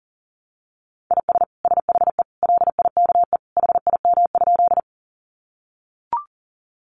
Morse code, heard as a keyed tone, about 40 wpm, 700 Hz. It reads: IS SHE LIKE HIM?
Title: Sentences from Top 100 Words 40wpm